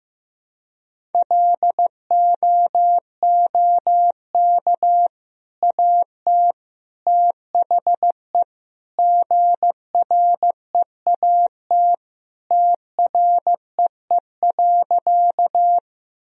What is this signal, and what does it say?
Morse code, heard as a keyed tone, about 15 wpm, 700 Hz. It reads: LOOK AT THE GREAT TREE.
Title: Sentences from Top 1000 Words - Review Part 1 - 15wpm